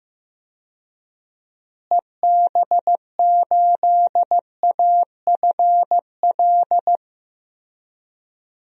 Morse code, heard as a keyed tone, about 15 wpm, 700 Hz. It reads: EB8AFL
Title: DXCC-100 Call Signs - 6 Characters - Repeated 2x - 15wpm